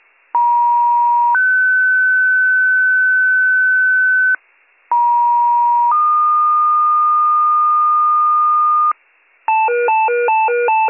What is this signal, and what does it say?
Two-tone page, heard as a one-second tone, then a three-second tone, about 950 then 1500 Hz, about 950 then 1220 Hz, with noise. It is heard from 0.3 to 8.9 s.